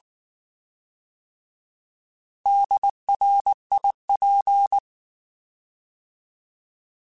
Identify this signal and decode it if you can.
Morse code, heard as a keyed tone, about 19 wpm, 800 Hz. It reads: DRIP
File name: Alphabetical / Alphabetical mp3s / Ds @18